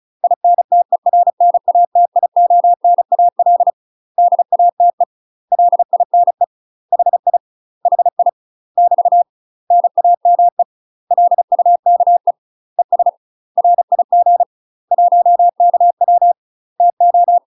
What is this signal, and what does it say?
Morse code, heard as a keyed tone, about 34 wpm, 700 Hz. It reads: INTERNATIONAL DATE LINE HI HI <BT> NAME LUKE ES RIG 1KW TO
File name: UK MASTER QSOs TRAINING/35WPM